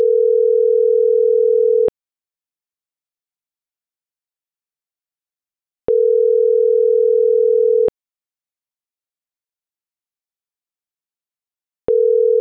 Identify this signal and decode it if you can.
Ringback tone (North America), 440 + 480 Hz.